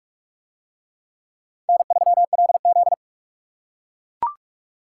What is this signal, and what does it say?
Morse code, heard as a keyed tone, about 44 wpm, 700 Hz. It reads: N3LZ